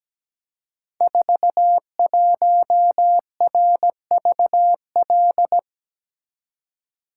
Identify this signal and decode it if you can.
Morse code, heard as a keyed tone, about 17 wpm, 700 Hz. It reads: 41RVL
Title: Code Groups - 5 Letter:Numbers - 17wpm